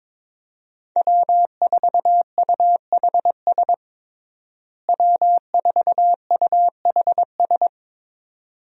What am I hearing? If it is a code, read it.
Morse code, heard as a keyed tone, about 22 wpm, 700 Hz. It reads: W4UHS W4UHS
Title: USA Call Signs - 5 Characters - Repeated 2x - 22wpm